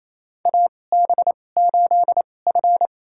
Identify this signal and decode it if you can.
Morse code, heard as a keyed tone, about 28 wpm, spaced as for 20 wpm, 700 Hz. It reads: AB8F
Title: USA Call Signs - 2x Character Spacing - 28wpm